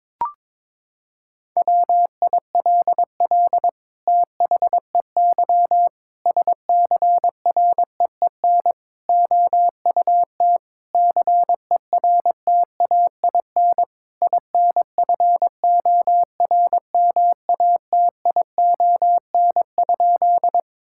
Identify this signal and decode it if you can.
Morse code, heard as a keyed tone, about 22 wpm, 700 Hz. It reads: WILL THEY SCREEN OUT CERTAIN INFORMATION?
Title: Sentences from Top 1000 Words - Review Part 6 - 22wpm